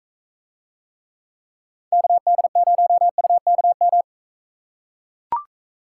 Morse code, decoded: KD0UKM

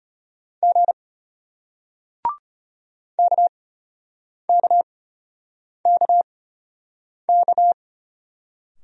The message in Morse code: G KKKK

